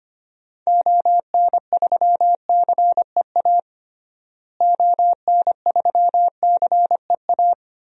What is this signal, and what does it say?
Morse code, heard as a keyed tone, about 25 wpm, 700 Hz. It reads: ON3CEA ON3CEA
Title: DXCC-100 Call Signs - 6 Characters - Repeated 3x - 25wpm